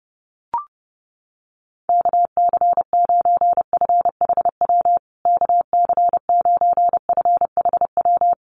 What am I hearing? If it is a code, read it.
Morse code, heard as a keyed tone, about 30 wpm, 700 Hz. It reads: KC9FHW KC9FHW
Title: USA Call Signs Repeated 3x 30wpm